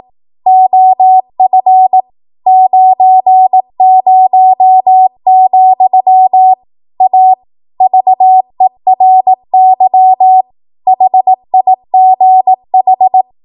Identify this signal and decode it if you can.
Morse code, heard as a keyed tone, about 18 wpm, 750 Hz. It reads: OF 90, A VERY HIGH
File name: Archive / 18 WPM / 18WPM